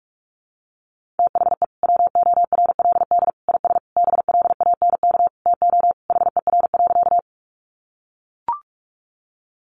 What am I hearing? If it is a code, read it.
Morse code, heard as a keyed tone, about 44 wpm, 700 Hz. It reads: THE WORLD IS BLANK TO HER.
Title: Sentences from Top 1500 Words 45wpm